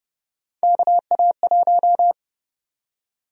Morse code, heard as a keyed tone, about 31 wpm, 700 Hz. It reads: KA1
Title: USA Call Sign Prefixes - Repeated 2x - 30wpm